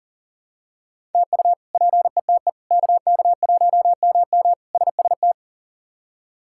Morse code, heard as a keyed tone, about 40 wpm, 700 Hz. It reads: TU PETE KK1MM SST
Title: K1USN SST QSO Practice - 40wpm